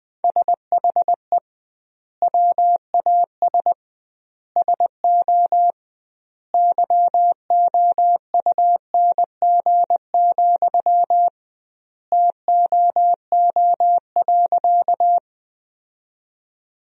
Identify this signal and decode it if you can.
Morse code, heard as a keyed tone, about 20 wpm, 700 Hz. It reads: SHE WAS SO YOUNG, TOO.
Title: 2x Word-Spacing - Sentences from Top 400 Words - 20wpm